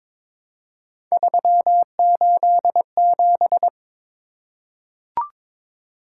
Morse code, heard as a keyed tone, about 22 wpm, 700 Hz. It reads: 387
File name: practice / 22wpm / Three Letter-Number Combinations - 22wpm